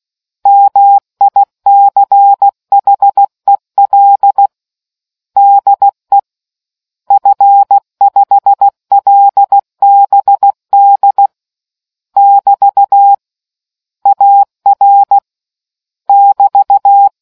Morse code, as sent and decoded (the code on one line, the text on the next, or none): -- .. -.-. .... .-.. / -.. / ..-. ..... .-.. -... -.. / -...- / .- .-. / -...-
MICHEL DE F5LBD <BT> AR <BT>